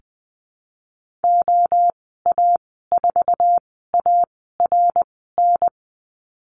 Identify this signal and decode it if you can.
Morse code, heard as a keyed tone, about 20 wpm, spaced as for 14 wpm, 700 Hz. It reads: OA4ARN
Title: 062 Lesson 11g - Letter h - Call Signs 20wpm